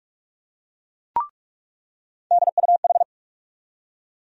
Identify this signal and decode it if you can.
Morse code, heard as a keyed tone, about 44 wpm, 700 Hz. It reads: DUH
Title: USA Call Sign Postfixes - 3 Letters 45wpm